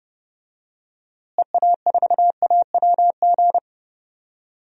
Morse code, heard as a keyed tone, about 30 wpm, 700 Hz. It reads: EA4AWG
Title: DXCC-100 Call Signs - 6 Characters - Repeated 3x - 30wpm